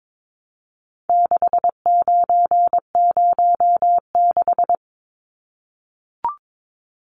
Morse code, heard as a keyed tone, about 22 wpm, 700 Hz. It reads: 6906
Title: Numbers - 4-Digit 22wpm